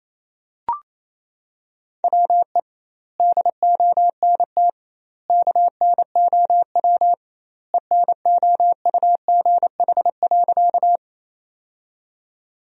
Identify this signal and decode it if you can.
Morse code, heard as a keyed tone, about 28 wpm, 700 Hz. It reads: WE DONT KNOW ENOUGH.
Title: 2x Word-Spacing - Sentences from Top 400 Words - 28wpm